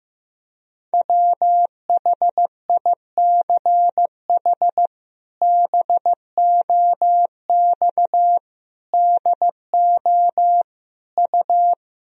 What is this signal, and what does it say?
Morse code, heard as a keyed tone, about 16 wpm, 700 Hz. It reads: WHICH BOX DO U
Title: Sentences from Top 500 Words and Abbreviations 15wpm